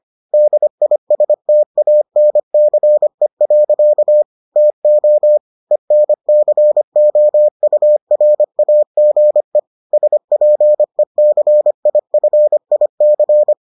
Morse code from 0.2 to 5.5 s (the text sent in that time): DISTANCE. TO